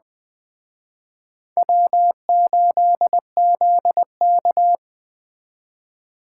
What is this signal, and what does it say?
Morse code, heard as a keyed tone, about 21 wpm, 700 Hz. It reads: W8ZK